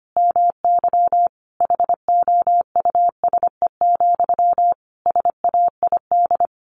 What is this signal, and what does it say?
Morse code, heard as a keyed tone, about 25 wpm, 700 Hz. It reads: MY HOUSE, SAID
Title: Sentences from Top 400 Words 25wpm